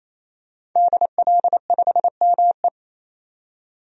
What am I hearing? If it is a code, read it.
Morse code, heard as a keyed tone, about 28 wpm, 700 Hz. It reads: DL5ME
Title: DXCC-100 Call Signs - 5 Characters - 28wpm